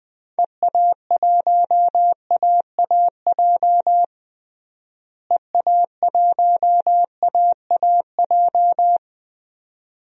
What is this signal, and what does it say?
Morse code, heard as a keyed tone, about 20 wpm, 700 Hz. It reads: EA1AAJ EA1AAJ
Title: DXCC-100 Call Signs - 6 Characters - Repeated 2x - 20wpm